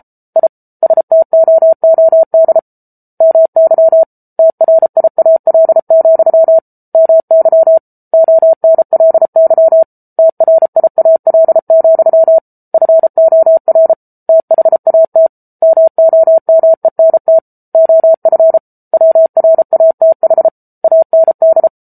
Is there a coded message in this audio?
Yes: Morse code from 0.4 to 21.7 s, heard as a keyed tone, about 34 wpm, 650 Hz. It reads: I STOOD MY TRIAL, MY ONLY TRIAL, FOR THAT MOMENT OF WRATH AND